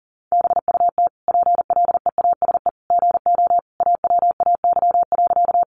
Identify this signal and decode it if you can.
Morse code, heard as a keyed tone, about 40 wpm, 700 Hz. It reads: BUT PLEASE GO AWAY.
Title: Sentences from Top 1500 Words 40wpm